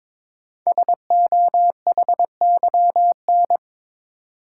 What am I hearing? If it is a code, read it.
Morse code, heard as a keyed tone, about 22 wpm, 700 Hz. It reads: SOHYN
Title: Code Groups - 5 Letters - 22wpm